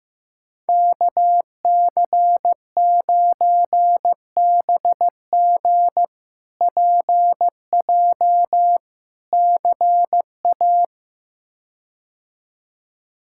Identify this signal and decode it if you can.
Morse code, heard as a keyed tone, about 15 wpm, 700 Hz. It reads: KC9BG PJ CA
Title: K1USN SST QSO Practice - 15wpm